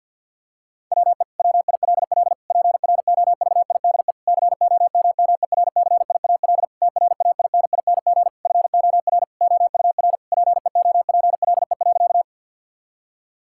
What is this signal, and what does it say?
Morse code, heard as a keyed tone, about 50 wpm, 700 Hz. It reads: WE WILL PROVIDE COMMERCIAL TRAINING FOR OUR PEOPLE.
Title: Sentences from Top 1000 Words - Review Part 5 - 50wpm